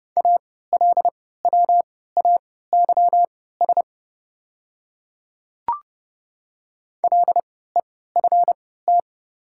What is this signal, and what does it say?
Morse code, heard as a keyed tone, about 30 wpm, spaced as for 17 wpm, 700 Hz. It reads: ALWAYS LEFT
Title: 3x Character Spacing - Top 300 Words - 30wpm